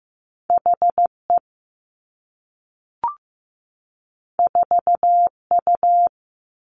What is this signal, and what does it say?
Morse code, heard as a keyed tone, about 15 wpm, 700 Hz. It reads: H E 4 U
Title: Two Challenging Letter-Number Combinations 15wpm